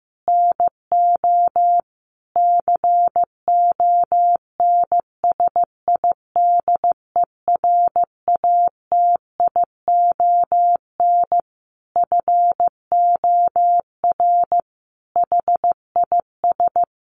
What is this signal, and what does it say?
Morse code, heard as a keyed tone, about 15 wpm, 700 Hz. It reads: NO CONSIDERATION FOR HIS
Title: Sentences from Top 1500 Words 15wpm